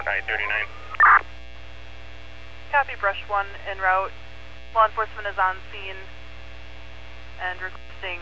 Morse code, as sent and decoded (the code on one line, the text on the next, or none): none